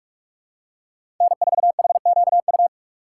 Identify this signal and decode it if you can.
Morse code, heard as a keyed tone, about 45 wpm, 700 Hz. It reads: N4HQU
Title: USA Call Signs - 5 Characters - 2x Character Spacing - 45wpm